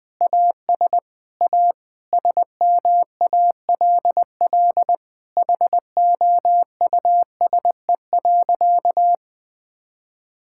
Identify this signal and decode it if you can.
Morse code, heard as a keyed tone, about 21 wpm, 700 Hz. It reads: AS A SMALL HOUSE.
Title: Sentences with 8-Word Phrases - 20wpm